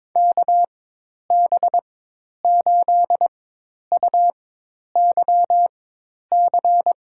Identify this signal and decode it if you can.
Morse code, heard as a keyed tone, about 22 wpm, spaced as for 10 wpm, 700 Hz. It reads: KB8UYC
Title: USA Call Signs - 4x Character Spacing - 22wpm